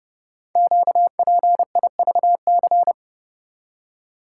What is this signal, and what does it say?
Morse code, heard as a keyed tone, about 30 wpm, 700 Hz. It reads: QPIVC
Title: Code Groups - 5 Letters - 30wpm